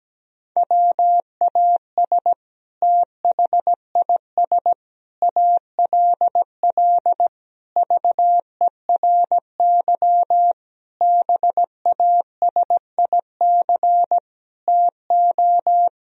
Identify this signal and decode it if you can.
Morse code, heard as a keyed tone, about 17 wpm, 700 Hz. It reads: WAS THIS ALL VERY BASIC TO
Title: Sentences from Top 1000 Words - Review Part 3 - 17wpm